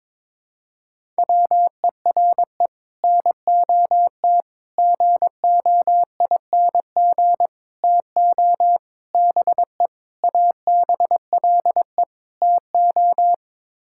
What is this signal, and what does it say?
Morse code, heard as a keyed tone, about 21 wpm, 700 Hz. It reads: WERE NOT GOING TO BE ABLE TO
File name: practice / 22wpm / Sets of 7 Words - 22wpm